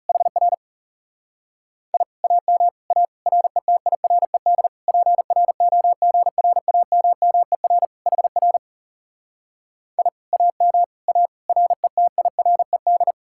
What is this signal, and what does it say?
Morse code, heard as a keyed tone, about 40 wpm, 700 Hz. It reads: HR I AM A RETIRED PROGRAMMER HR I AM A RETIRED